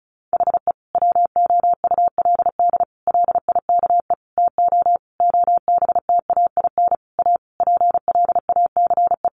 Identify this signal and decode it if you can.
Morse code, heard as a keyed tone, about 34 wpm, 700 Hz. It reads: HE WOULD LIKE TO OBTAIN A PLACE
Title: Sentences from Top 1300 Words 35wpm